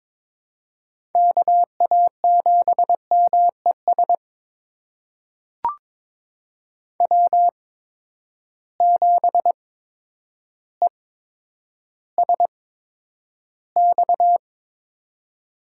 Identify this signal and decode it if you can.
Morse code, heard as a keyed tone, about 21 wpm, spaced as for 10 wpm, 700 Hz. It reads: KA7MES W7ESX
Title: USA Call Signs - 8x Character Spacing - 22wpm